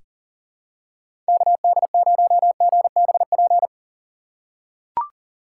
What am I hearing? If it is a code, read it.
Morse code, heard as a keyed tone, about 38 wpm, 700 Hz. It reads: KD0GBP
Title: USA Call Signs - 6 Characters - 4x Character Spacing - 40wpm